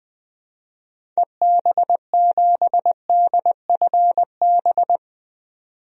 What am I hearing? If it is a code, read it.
Morse code, heard as a keyed tone, about 21 wpm, 700 Hz. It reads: EB7DFB
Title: DXCC-100 Call Signs - 6 Characters - 20wpm